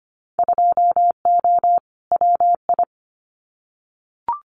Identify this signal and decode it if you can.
Morse code, heard as a keyed tone, about 25 wpm, 700 Hz. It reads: 2O WI
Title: Winter Field Day Exchanges 25wpm